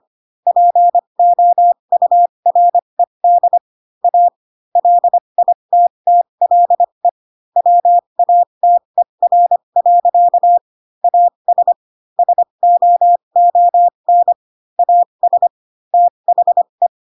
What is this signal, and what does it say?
Morse code, heard as a keyed tone, about 25 wpm, spaced as for 22 wpm, 700 Hz. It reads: POURED A LITTLE WATER. AS SOON AS THE